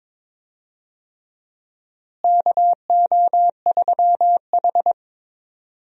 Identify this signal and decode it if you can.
Morse code, heard as a keyed tone, about 22 wpm, 700 Hz. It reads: KO3H
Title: USA Call Signs - 4 Characters - Repeated 3x - 22wpm